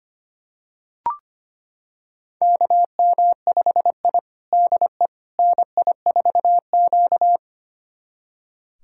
Morse code, heard as a keyed tone, about 25 wpm, 700 Hz. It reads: KM5I DE NI4Q
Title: USA Callsign DE Callsign - 4 Characters - 25wpm